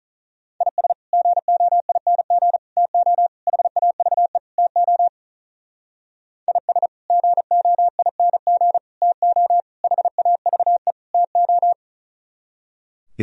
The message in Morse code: IS GOING TO HAVE TO IS GOING TO HAVE TO

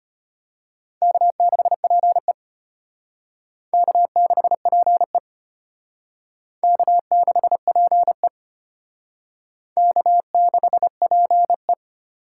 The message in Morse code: K6PE K6PE K6PE K6PE

